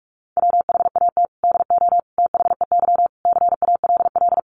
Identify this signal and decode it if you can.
Morse code, heard as a keyed tone, about 44 wpm, 700 Hz. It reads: WHAT DO THEY CALL
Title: Sentences from Top 100 Words 45wpm